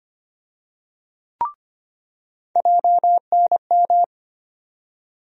Morse code, heard as a keyed tone, about 25 wpm, 700 Hz. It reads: JNM
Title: USA Call Sign Postfixes - 3 Letters 25wpm